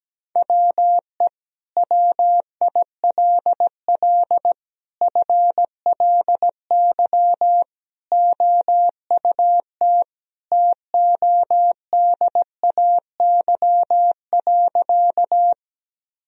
Morse code, heard as a keyed tone, about 17 wpm, 700 Hz. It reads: WE WILL FLY OUT TODAY.
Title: Sentences from Top 1000 Words - Review Part 2 - 17wpm